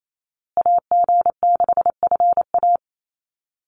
Morse code, heard as a keyed tone, about 28 wpm, 700 Hz. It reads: AG6FA
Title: USA Call Signs Repeated 3x 28wpm